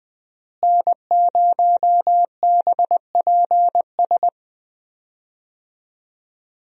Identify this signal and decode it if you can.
Morse code, heard as a keyed tone, about 20 wpm, 700 Hz. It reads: N0BPS
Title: USA Call Signs - 5 Characters - Repeated 2x - 20wpm